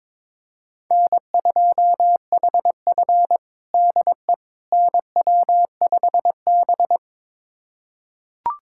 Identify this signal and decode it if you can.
Morse code, heard as a keyed tone, about 22 wpm, 700 Hz. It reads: N2HF DE NW5B
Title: USA Callsign DE Callsign - 4 Characters - 22wpm